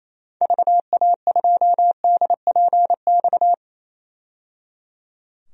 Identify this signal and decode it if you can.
Morse code, heard as a keyed tone, about 28 wpm, 700 Hz. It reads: VA2DPX
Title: DXCC-100 Call Signs - 6 Characters - 28wpm